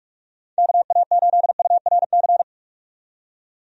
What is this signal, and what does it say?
Morse code, heard as a keyed tone, about 45 wpm, 700 Hz. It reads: KA8URC
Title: USA Call Signs - 6 Characters - 45wpm